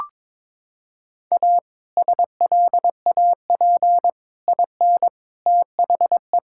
Morse code, heard as a keyed tone, about 22 wpm, 700 Hz. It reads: A SLAP IN THE
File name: practice / 22wpm / Sets of 6 Words - 22wpm